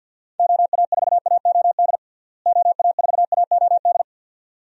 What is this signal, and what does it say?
Morse code, heard as a keyed tone, about 46 wpm, 700 Hz. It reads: OA4AOD OA4AOD
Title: USA Call Signs - 6 Characters - Repeated 2x - 50wpm